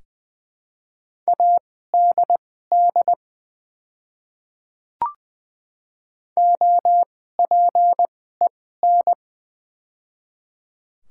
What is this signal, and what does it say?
Morse code, heard as a keyed tone, about 20 wpm, spaced as for 15 wpm, 700 Hz. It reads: ADD OPEN
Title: 2x Character Spacing - Top 300 Words - 20wpm